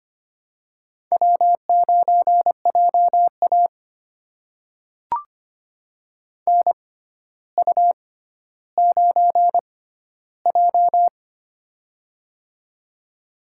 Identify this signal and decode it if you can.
Morse code, heard as a keyed tone, about 25 wpm, spaced as for 13 wpm, 700 Hz. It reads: W9JA NU9J
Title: USA Call Signs - 4 Characters - 6x Character Spacing - 25wpm